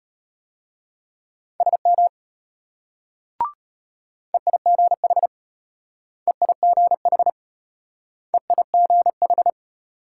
Morse code, heard as a keyed tone, about 34 wpm, 700 Hz. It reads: SM EIGH EIGH EIGH